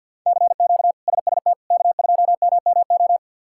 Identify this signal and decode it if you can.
Morse code, heard as a keyed tone, about 48 wpm, 690 Hz. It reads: CQ SST K2MMO
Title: K1USN SST QSO Practice - 50wpm